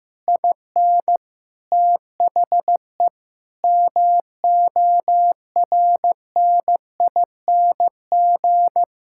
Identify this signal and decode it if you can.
Morse code, heard as a keyed tone, about 16 wpm, 700 Hz. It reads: IN THE MORNING
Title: Sentences from Top 1600 Words 15wpm